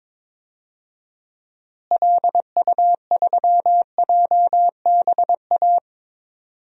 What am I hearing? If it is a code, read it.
Morse code, heard as a keyed tone, about 21 wpm, 700 Hz. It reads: LU3JBA
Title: DXCC-100 Call Signs - 6 Characters - Repeated 2x - 22wpm